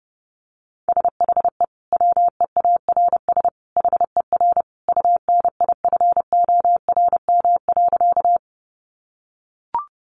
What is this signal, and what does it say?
Morse code, heard as a keyed tone, about 30 wpm, 700 Hz. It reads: SHE WEARS HER UNIFORM.
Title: Easy Phrases 30wpm